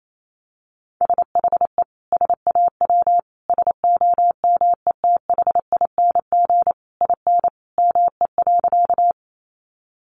Morse code, heard as a keyed tone, about 28 wpm, 700 Hz. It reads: SHE SAW SOMETHING IN ME.
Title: Sentences from Top 600 Words 28wpm